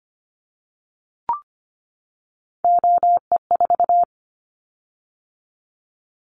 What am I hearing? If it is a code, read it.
Morse code, heard as a keyed tone, about 25 wpm, 700 Hz. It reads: OE4
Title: Callsign Prefixes for Top 25 DXCC - 25wpm